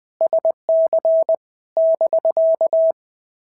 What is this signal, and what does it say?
Morse code, heard as a keyed tone, about 20 wpm, 650 Hz. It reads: SC <BK>
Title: POTA QSO Practice - 20wpm